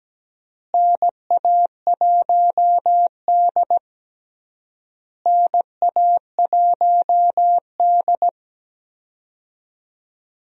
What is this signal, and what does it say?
Morse code, heard as a keyed tone, about 17 wpm, 700 Hz. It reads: NA1D NA1D
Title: USA Call Signs - 4 Characters - Repeated 3x - 17wpm